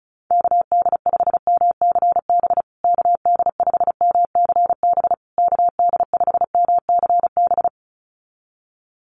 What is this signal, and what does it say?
Morse code, heard as a keyed tone, about 34 wpm, 700 Hz. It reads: KD5MCB KD5MCB KD5MCB
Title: USA Call Signs Repeated 3x 35wpm